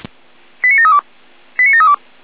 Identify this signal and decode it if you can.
Five-tone selective call (ZVEI), heard as quick runs of five tones, tones of about 80 ms. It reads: 89732 89732